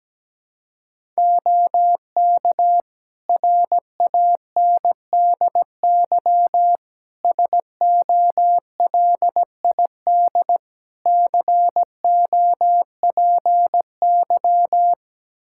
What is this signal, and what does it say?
Morse code, heard as a keyed tone, about 17 wpm, 700 Hz. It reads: OK RANDY SOLID COPY
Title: Practice QSO - Exchange 3 - More Info - 17wpm